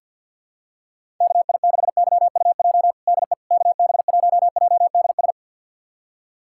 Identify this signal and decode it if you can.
Morse code, heard as a keyed tone, about 50 wpm, 700 Hz. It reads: KI6YUJ DE KB1JDS
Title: USA Callsign DE Callsign 50wpm